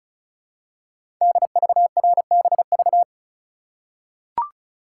Morse code, heard as a keyed tone, about 36 wpm, 700 Hz. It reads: DVRBV
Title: Code Groups - 5 Letters - 35wpm